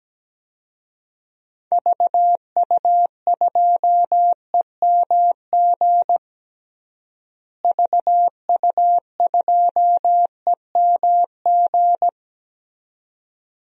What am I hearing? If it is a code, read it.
Morse code, heard as a keyed tone, about 17 wpm, 700 Hz. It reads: VU2EMG VU2EMG
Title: DXCC-100 Call Signs - 6 Characters - Repeated 2x - 17wpm